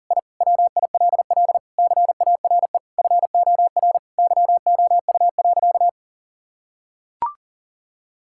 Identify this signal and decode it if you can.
Morse code, heard as a keyed tone, about 40 wpm, 700 Hz. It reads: I WILL CARE FOR YOU.
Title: Sentences from Top 1000 Words - Review Part 6 - 40wpm